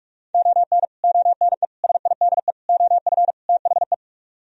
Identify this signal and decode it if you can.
Morse code, heard as a keyed tone, about 45 wpm, 700 Hz. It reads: ON ONE SIDE OF THE